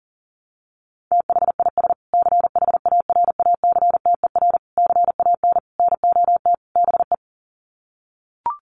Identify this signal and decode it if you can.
Morse code, heard as a keyed tone, about 40 wpm, 700 Hz. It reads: THIS CHARACTER CAN NOT BE
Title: Sets of 5 Words Encore2 40wpm